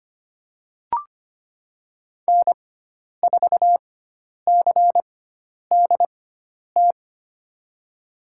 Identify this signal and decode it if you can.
Morse code, heard as a keyed tone, about 25 wpm, spaced as for 10 wpm, 700 Hz. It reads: N4CDT
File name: Lesson 16g - Letter c - Call Signs 25wpm